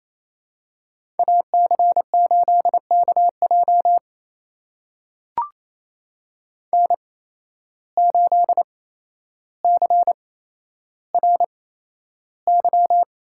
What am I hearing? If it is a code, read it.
Morse code, heard as a keyed tone, about 28 wpm, spaced as for 12 wpm, 700 Hz. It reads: AC8KJ N8CRY
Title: USA Call Signs - 8x Character Spacing - 28wpm